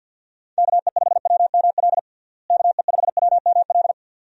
Morse code, heard as a keyed tone, about 50 wpm, 700 Hz. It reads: KE5WML KE5WML
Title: USA Call Signs - 6 Characters - Repeated 2x - 50wpm